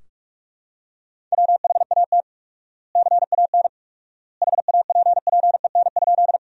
Morse code, heard as a keyed tone, about 43 wpm, 700 Hz. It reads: WHAT CAN HAPPEN?
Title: 4x Word-Spacing - Sentences from Top 400 Words - 45wpm